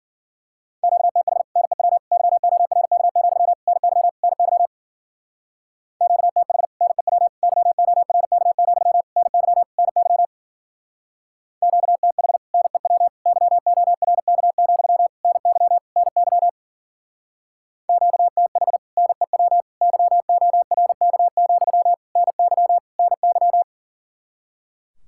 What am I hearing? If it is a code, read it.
Morse code, heard as a keyed tone, about 50 wpm, 700 Hz. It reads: QTH NEW YORK, NY NY QTH NEW YORK, NY NY QTH NEW YORK, NY NY QTH NEW YORK, NY NY
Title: Speed-Racing Common QSO Phrases Encore - 40wpm